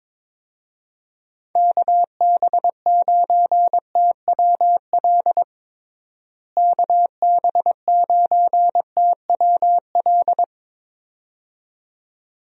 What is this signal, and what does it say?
Morse code, heard as a keyed tone, about 22 wpm, 700 Hz. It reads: KB9TWL KB9TWL
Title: USA Call Signs - 6 Characters - Repeated 2x - 22wpm